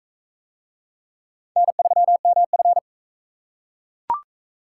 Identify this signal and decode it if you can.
Morse code, heard as a keyed tone, about 40 wpm, 700 Hz. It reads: N3MF